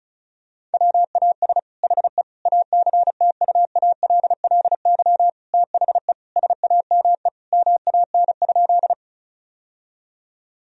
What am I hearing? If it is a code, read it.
Morse code, heard as a keyed tone, about 36 wpm, 700 Hz. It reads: WAS HE ACTUALLY THE SAME MAN?
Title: Sentences from Top 1000 Words - Review Part 4 - 35wpm